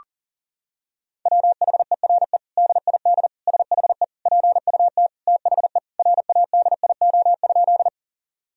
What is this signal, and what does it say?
Morse code, heard as a keyed tone, about 40 wpm, 700 Hz. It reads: WHERE DID SHE PUT THE RADIO?
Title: Sentences from Top 1000 Words - Review Part 3 - 40wpm